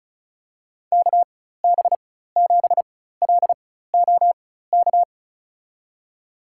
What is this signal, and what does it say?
Morse code, heard as a keyed tone, about 36 wpm, spaced as for 16 wpm, 700 Hz. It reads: KB7LOK
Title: USA Call Signs - 6 Characters - 4x Character Spacing - 35wpm